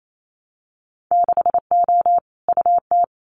Morse code, heard as a keyed tone, about 28 wpm, 700 Hz. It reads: 6O UT
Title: Winter Field Day Exchanges Encore 28wpm